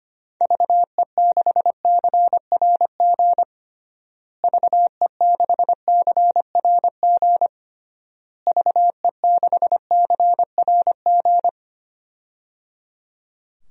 Morse code, heard as a keyed tone, about 25 wpm, 700 Hz. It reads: VE6CRG VE6CRG VE6CRG
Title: DXCC-100 Call Signs - 6 Characters - Repeated 3x - 25wpm